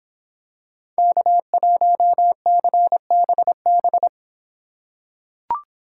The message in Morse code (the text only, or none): K1CBB